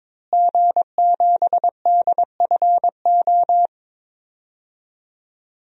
Morse code, heard as a keyed tone, about 22 wpm, 700 Hz. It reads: G7DFO
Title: DXCC-100 Call Signs - 5 Characters - Repeated 2x - 22wpm